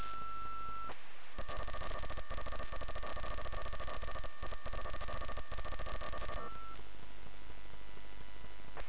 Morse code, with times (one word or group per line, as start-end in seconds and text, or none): none